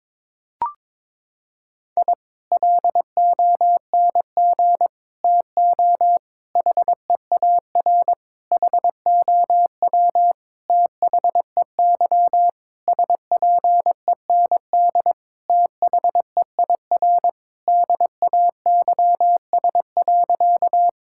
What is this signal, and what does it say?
Morse code, heard as a keyed tone, about 22 wpm, 700 Hz. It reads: I LONG TO HEAR HOW THEY SPEND THEIR DAYS.
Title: Sentences from Top 1000 Words - Review Part 5 - 22wpm